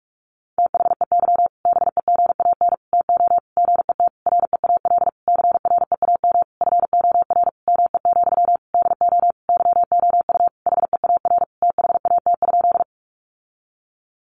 Morse code, heard as a keyed tone, about 44 wpm, 700 Hz. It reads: THEY BEGAN TO GET REAL CREAM FOR ME, DO YOU HEAR THAT?
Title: Sentences from Top 1500 Words 45wpm